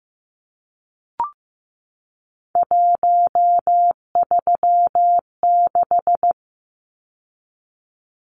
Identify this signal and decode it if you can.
Morse code, heard as a keyed tone, about 15 wpm, 700 Hz. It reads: 136
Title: Numbers - 3-Digit 15wpm